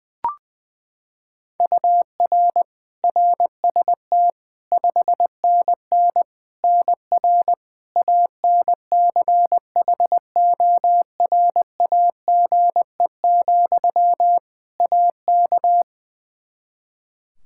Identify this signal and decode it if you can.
Morse code, heard as a keyed tone, about 20 wpm, 700 Hz. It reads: UR RST 5NN NR ANCHORAGE, AK
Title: Practice QSO - Exchange 1 - Beginning - 20wpm